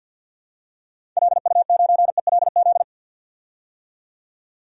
Morse code, heard as a keyed tone, about 51 wpm, 700 Hz. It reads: LU9ELZ
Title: DXCC-100 Call Signs - 6 Characters - 50wpm